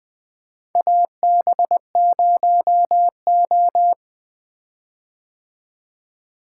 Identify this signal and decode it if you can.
Morse code, heard as a keyed tone, about 20 wpm, 700 Hz. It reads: AB0O
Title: 2023 POTA Activator Call Signs - 20wpm